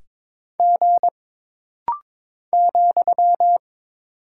Morse code, heard as a keyed tone, about 22 wpm, 700 Hz. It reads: G ,
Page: This rapid-fire timing is geared to push your ICR (Instant Character Recognition) skills.